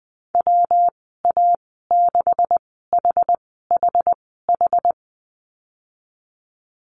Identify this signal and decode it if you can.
Morse code, heard as a keyed tone, about 20 wpm, spaced as for 15 wpm, 700 Hz. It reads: WA6HHH
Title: 161 Lesson 26g - Letter g - Call Signs 20wpm